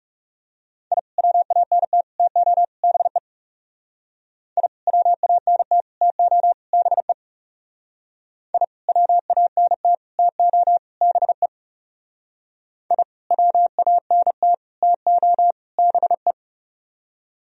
Morse code, heard as a keyed tone, about 36 wpm, 700 Hz. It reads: I WANT TO BE I WANT TO BE I WANT TO BE I WANT TO BE
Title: Speed-Racing - Sets of 4 Words 30wpm